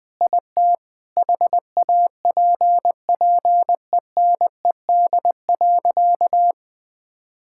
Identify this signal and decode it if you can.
Morse code, heard as a keyed tone, about 20 wpm, 700 Hz. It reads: IT HAPPENED.
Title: Sentences with 5-Word Phrases - Spoken Beginning - 20wpm